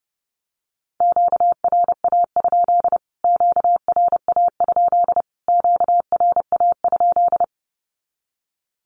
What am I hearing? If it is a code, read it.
Morse code, heard as a keyed tone, about 30 wpm, 700 Hz. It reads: QRA? QRA? QRA?
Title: Q-Codes Repeated 3x 30wpm